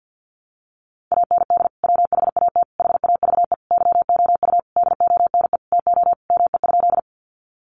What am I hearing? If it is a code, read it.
Morse code, heard as a keyed tone, about 50 wpm, 700 Hz. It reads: AND WHAT HAVE YOU DONE TO ME?
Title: Sentences from Top 500 Words 50wpm